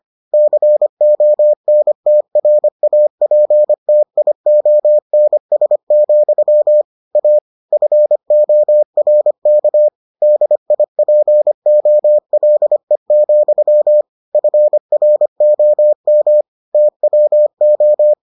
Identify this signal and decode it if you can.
Morse code, heard as a keyed tone, about 25 wpm, 600 Hz. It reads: CONTRAPTIONS, A FORK DIPOLE, FROM TWO